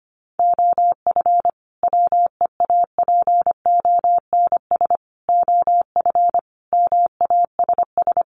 Morse code, heard as a keyed tone, about 25 wpm, 700 Hz. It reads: OF WEAPONS OF MASS
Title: Sets of 5 Words Encore 25wpm